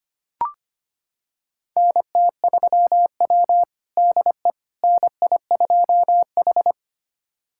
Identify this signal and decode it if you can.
Morse code, heard as a keyed tone, about 25 wpm, 700 Hz. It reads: NT3W DE NI2H
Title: USA Callsign DE Callsign - 4 Characters - 25wpm